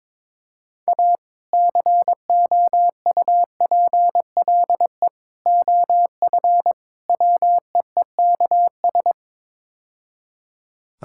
Morse code, heard as a keyed tone, about 21 wpm, 700 Hz. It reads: A COUPLE OF WEEKS
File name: practice / 22wpm / Build A Phrase - 5-Word Phrases - 22wpm